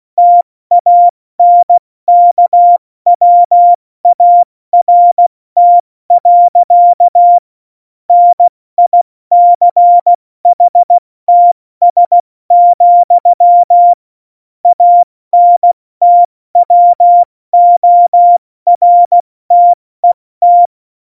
Morse code, heard as a keyed tone, about 16 wpm, 700 Hz. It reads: TANKWART. NICHTS, ANTWORTET
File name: Morsekurs / Witze / Witz 11 August 2020, 70 F80